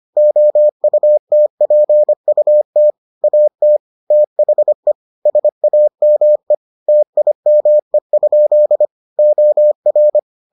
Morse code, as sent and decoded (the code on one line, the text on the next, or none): --- ..- - .--. ..- - / .- - / - .... / ... .- -- . / - .. -- . ..--.. / --- .-.
OUTPUT AT THE SAME TIME? OR